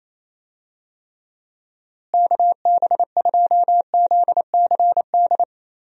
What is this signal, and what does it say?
Morse code, heard as a keyed tone, about 27 wpm, 700 Hz. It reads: KB2ZCD
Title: USA Call Signs - 6 Characters - Repeated 2x - 28wpm